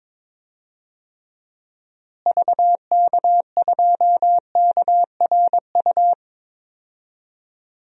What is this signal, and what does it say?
Morse code, heard as a keyed tone, about 21 wpm, 700 Hz. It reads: VK2KRU
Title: DXCC-100 Call Signs - 6 Characters - Repeated 2x - 22wpm